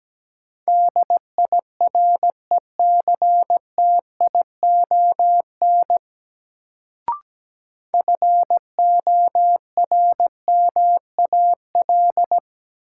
Morse code, heard as a keyed tone, about 17 wpm, 700 Hz. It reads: DIRECTION FORMAL
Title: Sentences from Top 1300 Words - New Word Drill Alt - 17wpm